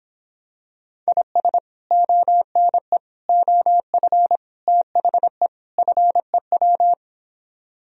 Morse code, heard as a keyed tone, about 26 wpm, 700 Hz. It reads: IS ONE OF THE FEW